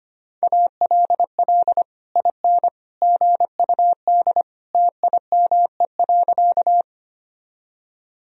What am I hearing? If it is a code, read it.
Morse code, heard as a keyed tone, about 25 wpm, 700 Hz. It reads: ALL IN GUD TIME.
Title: Sentences from Top 200 Words and Abbreviations 25wpm